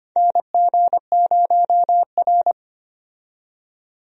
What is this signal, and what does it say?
Morse code, heard as a keyed tone, about 25 wpm, 700 Hz. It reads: NG0R
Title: USA Call Signs - 4 Characters - 25wpm